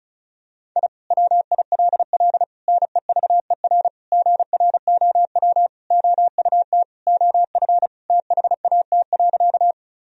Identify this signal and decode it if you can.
Morse code, heard as a keyed tone, about 34 wpm, 700 Hz. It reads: I WILL NEVER GROW OUT OF THAT.
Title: Sentences from Top 1000 Words - Review Part 1 - 35wpm